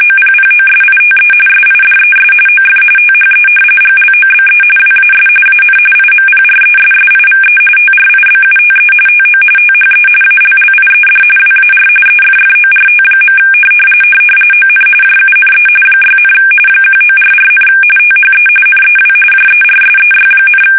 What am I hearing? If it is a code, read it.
RTTY, about 45.45 baud. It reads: OPPOSITE SIDES OF THE GLOBE WAS AN IMPRESSIVE TRIBUTE TO THE ALLIED WAR EFFORT. ON JUNE 6, 1944, THE NEW JERSEY,
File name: fsk850 / battleshipnj